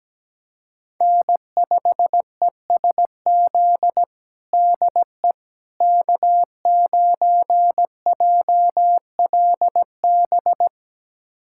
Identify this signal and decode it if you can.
Morse code, heard as a keyed tone, about 17 wpm, 700 Hz. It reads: N5ESZ DE K9JLB
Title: USA Callsign DE Callsign - 5 Characters - 17wpm